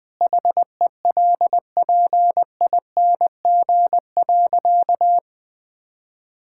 Morse code, heard as a keyed tone, about 19 wpm, 700 Hz. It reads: HELPING.
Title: Sentences with 3-Word Phrases - Spoken Beginning - 20wpm